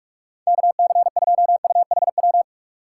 Morse code, heard as a keyed tone, about 46 wpm, 700 Hz. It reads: KC2UHW
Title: USA Call Signs - 6 Characters - 2x Character Spacing - 45wpm